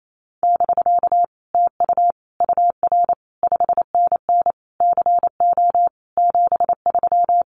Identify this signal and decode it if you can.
Morse code, heard as a keyed tone, about 28 wpm, 700 Hz. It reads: <BK> TU UR 5NN CO 73
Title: Common QSO Phrases Encore 28wpm